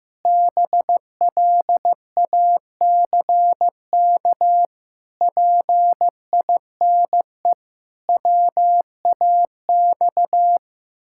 Morse code, heard as a keyed tone, about 16 wpm, 700 Hz. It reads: BLACK PINE WAX
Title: Pangram Challenge v2 - 15wpm